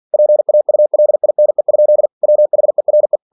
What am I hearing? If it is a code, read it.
Morse code, heard as a keyed tone, about 46 wpm, 600 Hz. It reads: PAULINE? WHERE
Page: Never 47wpm